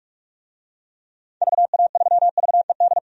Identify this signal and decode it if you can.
Morse code, heard as a keyed tone, about 41 wpm, 700 Hz. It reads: VA3VED
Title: DXCC-100 Call Signs - 6 Characters - 45wpm